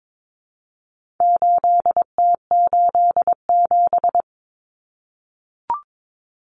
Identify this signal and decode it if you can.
Morse code, heard as a keyed tone, about 22 wpm, 700 Hz. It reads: 8T87
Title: Common Cut Numbers - 4-Digit 22wpm